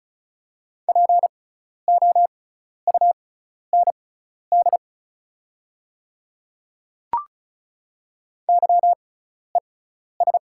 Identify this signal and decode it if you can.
Morse code, heard as a keyed tone, about 36 wpm, spaced as for 12 wpm, 700 Hz. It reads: POUND YES
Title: 6x Character Spacing - Top 500 Words - 35wpm